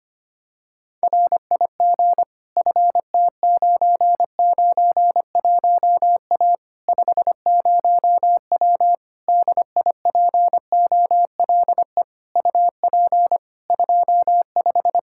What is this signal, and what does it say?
Morse code, heard as a keyed tone, about 25 wpm, 700 Hz. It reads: RIG FT991A 50W DIPOLE UP 25
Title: Practice QSO - Exchange 2 - Info - 25wpm